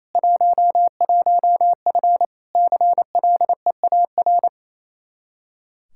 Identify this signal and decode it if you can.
Morse code, heard as a keyed tone, about 28 wpm, 700 Hz. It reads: 11F CLEAR
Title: Practice QSO - Exchange 2 - Info - 28wpm